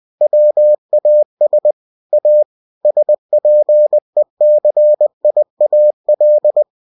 Morse code, heard as a keyed tone, about 20 wpm, 600 Hz. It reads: WAS A SPECIAL
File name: cw-books / winnie-the-pooh / winnie-20wpm-page-0109